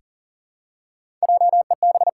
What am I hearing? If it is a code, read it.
Morse code, heard as a keyed tone, about 40 wpm, 700 Hz. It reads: JEB
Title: Pangram Challenge v2 - 40wpm